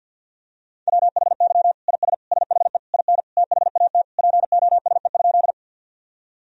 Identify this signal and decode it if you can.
Morse code, heard as a keyed tone, about 53 wpm, 700 Hz. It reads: WHY IS SHE IN THAT POSE?
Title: Sentences from Top 1000 Words - Review Part 2 - 50wpm